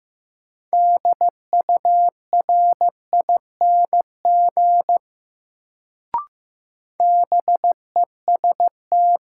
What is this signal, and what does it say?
Morse code, heard as a keyed tone, about 15 wpm, 700 Hz. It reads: DURING BEST
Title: Sets of 2 Words - New Word Drill - 15wpm